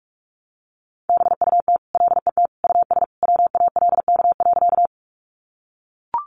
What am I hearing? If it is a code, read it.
Morse code, heard as a keyed tone, about 44 wpm, 700 Hz. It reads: BUT LET US WALK.
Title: Sentences from Top 300 Words 45wpm